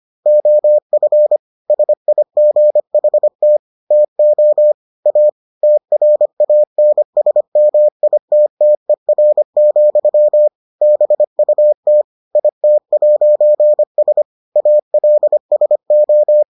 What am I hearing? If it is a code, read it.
Morse code, heard as a keyed tone, about 25 wpm, 600 Hz. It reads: OF SIGHT TO A TRANSMITTER, BUT IT'S ALSO